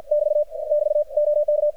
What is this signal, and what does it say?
Morse code, heard as a keyed tone, about 57 wpm, 600 Hz. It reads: <BT> <BT> OK
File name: archived / CRSnet